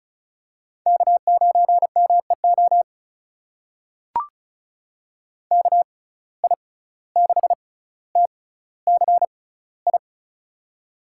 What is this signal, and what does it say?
Morse code, heard as a keyed tone, about 36 wpm, spaced as for 17 wpm, 700 Hz. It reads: K9MEO KI6TCI
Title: USA Call Signs - 6x Character Spacing - 35wpm